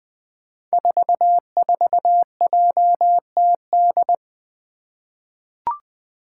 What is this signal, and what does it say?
Morse code, heard as a keyed tone, about 21 wpm, 700 Hz. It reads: 44JTD
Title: Five Challenging Letter-Number Combinations 20wpm